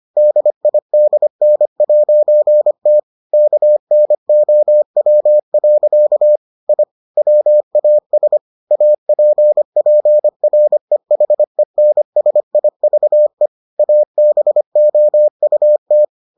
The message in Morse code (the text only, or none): DIDN'T KNOW. I WAS APPREHENSIVE ABOUT